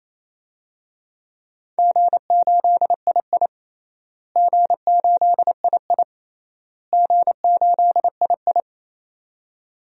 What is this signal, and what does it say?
Morse code, heard as a keyed tone, about 27 wpm, 700 Hz. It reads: G8II G8II G8II
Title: DXCC-100 Call Signs - 4 Characters - Repeated 3x - 28wpm